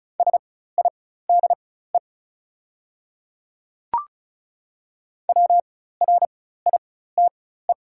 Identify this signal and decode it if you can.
Morse code, heard as a keyed tone, about 36 wpm, spaced as for 16 wpm, 700 Hz. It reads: SIDE WRITE
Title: 4x Character Spacing - Top 100 Words - 35wpm